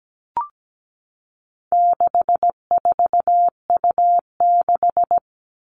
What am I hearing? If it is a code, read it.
Morse code, heard as a keyed tone, about 17 wpm, 700 Hz. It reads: 64U6
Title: All Cut Numbers - 4-Digit 17wpm